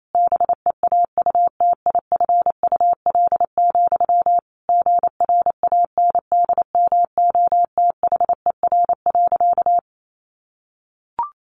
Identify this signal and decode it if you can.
Morse code, heard as a keyed tone, about 28 wpm, 700 Hz. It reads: BEAUTIFUL, GRANDMOTHER.
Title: Sentences from Top 1200 Words 28wpm